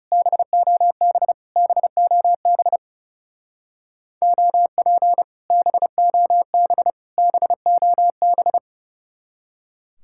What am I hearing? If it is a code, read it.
Morse code, heard as a keyed tone, about 36 wpm, 700 Hz. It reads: BOB BOB OP BOB BOB
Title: Speed-Racing Common QSO Phrases Encore - 30wpm